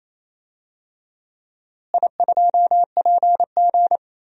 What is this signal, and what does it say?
Morse code, heard as a keyed tone, about 28 wpm, 700 Hz. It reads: I2PG
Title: DXCC-100 Call Signs - 4 Characters - Repeated 3x - 28wpm